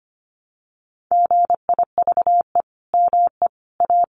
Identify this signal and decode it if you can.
Morse code, heard as a keyed tone, about 25 wpm, 700 Hz. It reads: GIVE ME A
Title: Sentences from Top 200 Words 25wpm